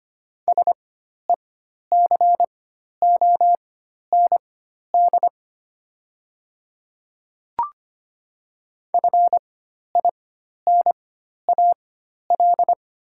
Morse code, heard as a keyed tone, about 25 wpm, spaced as for 12 wpm, 690 Hz. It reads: SECOND FINAL
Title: 4x Character Spacing - Top 500 Words - 25wpm